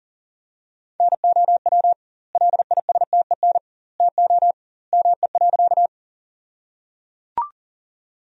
Morse code, heard as a keyed tone, about 37 wpm, 700 Hz. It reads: NOW LISTEN TO ME.
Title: 2x Word-Spacing - Sentences from Top 400 Words - 40wpm